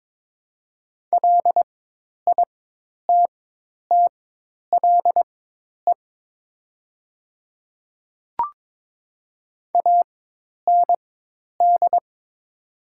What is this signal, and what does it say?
Morse code, heard as a keyed tone, about 22 wpm, spaced as for 10 wpm, 700 Hz. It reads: LITTLE AND
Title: 4x Character Spacing - Top 300 Words - 22wpm